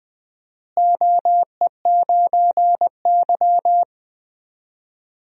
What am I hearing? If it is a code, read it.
Morse code, heard as a keyed tone, about 20 wpm, 700 Hz. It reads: OE9Y
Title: DXCC-100 Call Signs - 4 Characters - Repeated 2x - 20wpm